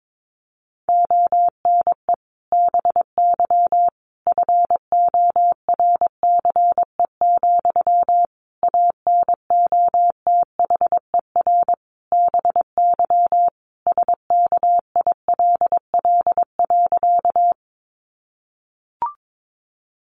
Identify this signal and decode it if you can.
Morse code, heard as a keyed tone, about 22 wpm, 700 Hz. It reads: ONE BY FORCE, ANOTHER BY SKILL.
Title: Sentences from Top 1200 Words 22wpm